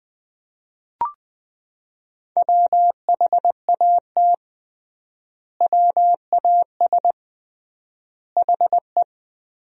Morse code, heard as a keyed tone, about 20 wpm, 700 Hz. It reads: WHAT WAS HE
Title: 3x Word-Spacing - Sentences from Top 100 Words - 20wpm